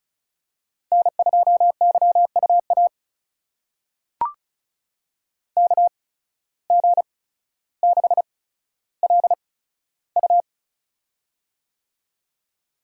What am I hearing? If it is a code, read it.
Morse code, heard as a keyed tone, about 36 wpm, spaced as for 15 wpm, 700 Hz. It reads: N2YUA KG6LU
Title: USA Call Signs - 8x Character Spacing - 35wpm